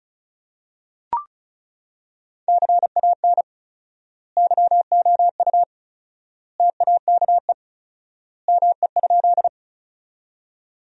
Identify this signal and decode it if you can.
Morse code, heard as a keyed tone, about 36 wpm, 700 Hz. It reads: CAN YOU TAKE ME?
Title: 4x Word-Spacing - Sentences from Top 200 Words - 35wpm